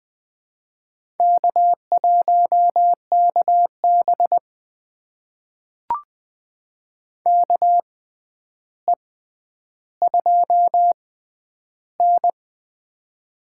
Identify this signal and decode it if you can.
Morse code, heard as a keyed tone, about 20 wpm, spaced as for 10 wpm, 700 Hz. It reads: K1KB KE2N
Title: USA Call Signs - 4 Characters - 6x Character Spacing - 20wpm